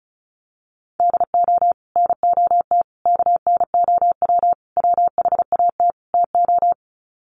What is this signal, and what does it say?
Morse code, heard as a keyed tone, about 34 wpm, 700 Hz. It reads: DO NOT KNOW WHAT TO
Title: Sets of 5 Words 35wpm